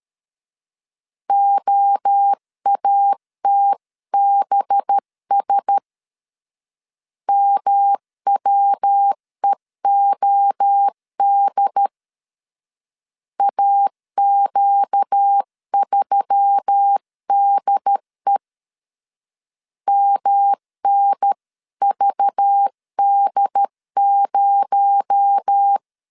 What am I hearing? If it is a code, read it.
Morse code, heard as a keyed tone, about 13 wpm, 800 Hz. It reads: OATBS MWEOD AQ3DE MNVD0
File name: CW / DET / 1 6 1500C S C 12wpm 3533